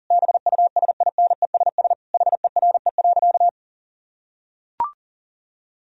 Morse code, heard as a keyed tone, about 40 wpm, 700 Hz. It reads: BUSINESS HERE.
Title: Sentences from Top 1000 Words - Review Part 4 - 40wpm